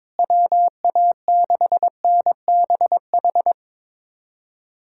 Morse code, heard as a keyed tone, about 22 wpm, 700 Hz. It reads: WA6NBH
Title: USA Callsign DE Callsign - 6 Characters - 22wpm